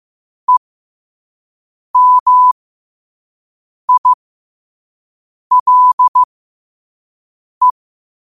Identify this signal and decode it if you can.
Morse code, heard as a keyed tone, about 15 wpm, spaced as for 5 wpm, 1000 Hz. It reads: EMILE
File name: qotdfiles / 5wpm